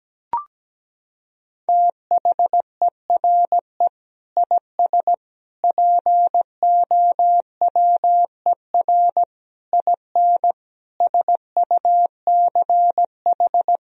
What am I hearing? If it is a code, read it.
Morse code, heard as a keyed tone, about 17 wpm, 700 Hz. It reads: THERE IS POWER IN SUCH
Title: Sentences from Top 1000 Words - Review Part 4 - 17wpm